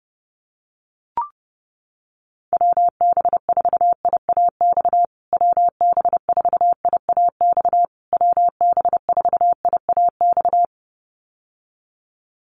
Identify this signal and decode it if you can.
Morse code, heard as a keyed tone, about 28 wpm, 700 Hz. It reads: WB4IAX WB4IAX WB4IAX
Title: USA Call Signs Repeated 3x 30wpm